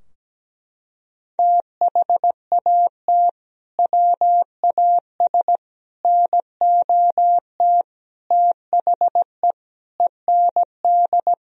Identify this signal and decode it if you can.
Morse code, heard as a keyed tone, about 17 wpm, 700 Hz. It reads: THAT WAS NOT THE END
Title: Sentences from Top 1000 Words - Review Part 2 - 17wpm